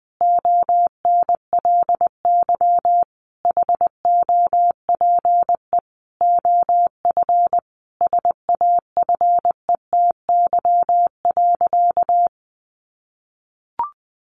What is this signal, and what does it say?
Morse code, heard as a keyed tone, about 20 wpm, 700 Hz. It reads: ONLY HOPE OF SAFETY.
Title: Sentences from Top 1000 Words 20wpm